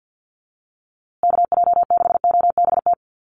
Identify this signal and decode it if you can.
Morse code, heard as a keyed tone, about 48 wpm, 700 Hz. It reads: KJ6OBT
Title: USA Call Signs Repeated 3x 50wpm